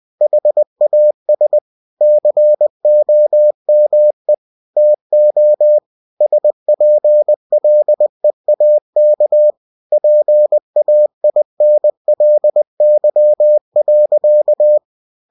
Morse code, heard as a keyed tone, about 20 wpm, 600 Hz. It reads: HAS COME TO SPLEAK PAINLY.
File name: cw-books / winnie-the-pooh / winnie-20wpm-page-0071